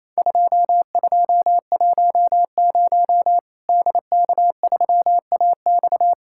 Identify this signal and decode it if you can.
Morse code, heard as a keyed tone, about 28 wpm, 700 Hz. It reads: 2210 DK3AX
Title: Worked All Europe DX Contest - QTC Exchange - 28wpm